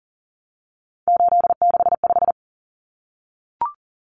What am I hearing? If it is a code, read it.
Morse code, heard as a keyed tone, about 40 wpm, 700 Hz. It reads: 865
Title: Common Cut Numbers - 3-Digit 40wpm